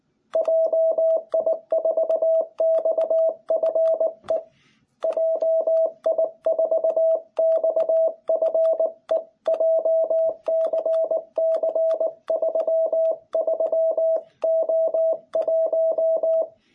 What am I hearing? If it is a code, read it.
Morse code, heard as a keyed tone, about 19 wpm, 660 Hz. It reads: JI4XFE JI4XFE JCC33O1
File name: jh8cbh / 2021cw